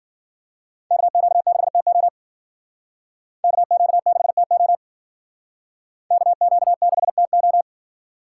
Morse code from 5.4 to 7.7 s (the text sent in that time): KQ6TO